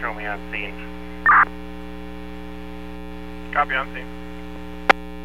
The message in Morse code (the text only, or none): none